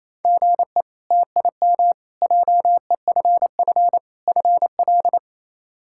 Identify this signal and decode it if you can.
Morse code, heard as a keyed tone, about 28 wpm, 700 Hz. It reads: GE TIM JEFF FL
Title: K1USN SST QSO Practice - 28wpm